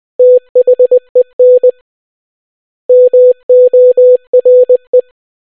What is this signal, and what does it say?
Morse code, heard as a keyed tone, about 20 wpm, 500 Hz. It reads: THEN MORE